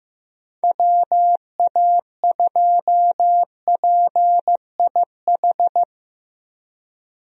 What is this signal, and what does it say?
Morse code, heard as a keyed tone, about 16 wpm, 700 Hz. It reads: WA2PIH
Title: USA Call Signs - 6 Characters - Repeated 3x - 15wpm